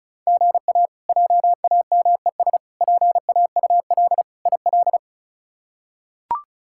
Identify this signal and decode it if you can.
Morse code, heard as a keyed tone, about 34 wpm, 700 Hz. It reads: GA JAMES PAUL IL
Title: K1USN SST QSO Practice - 35wpm